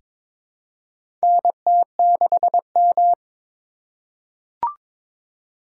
Morse code, heard as a keyed tone, about 23 wpm, 700 Hz. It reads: NT6M
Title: USA Call Signs - 4 Characters - 4x Character Spacing - 22wpm